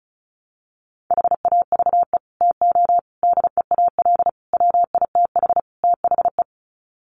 Morse code, heard as a keyed tone, about 34 wpm, 700 Hz. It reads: HAVE TO DEAL WITH THE
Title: Sets of 5 Words Encore2 35wpm